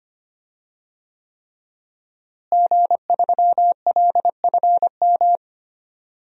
Morse code, heard as a keyed tone, about 25 wpm, 700 Hz. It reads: G3LFM
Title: DXCC-100 Call Signs - 5 Characters - Repeated 3x - 25wpm